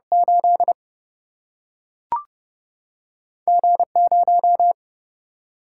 Morse code, heard as a keyed tone, about 30 wpm, 700 Hz. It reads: 8 G0
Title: Single-Letter Single-Number 30wpm